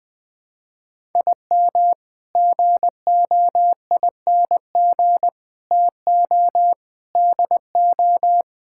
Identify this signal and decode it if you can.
Morse code, heard as a keyed tone, about 20 wpm, 700 Hz. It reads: IM GOING TO DO
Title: Sentences with 8-Word Phrases - Spoken Beginning - 20wpm